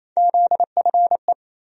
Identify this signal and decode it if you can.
Morse code, heard as a keyed tone, about 27 wpm, 700 Hz. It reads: ZFE
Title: USA Call Sign Postfixes - 3 Letters 28wpm